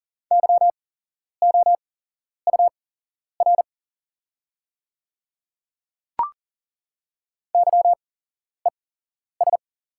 Morse code, heard as a keyed tone, about 40 wpm, spaced as for 11 wpm, 700 Hz. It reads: YOUR YES
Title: 8x Character Spacing - Top 500 Words - 40wpm